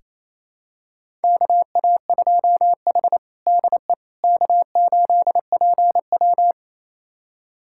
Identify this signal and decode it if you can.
Morse code, heard as a keyed tone, about 28 wpm, 700 Hz. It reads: KA2H DE K8PW
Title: USA Callsign DE Callsign - 4 Characters - 28wpm